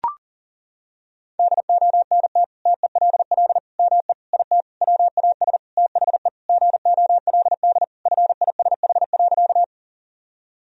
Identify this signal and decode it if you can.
Morse code, heard as a keyed tone, about 40 wpm, 700 Hz. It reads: DONT TELL ME IT WAS THE GOLD FISH.
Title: Sentences from Top 1000 Words - Review Part 3 - 40wpm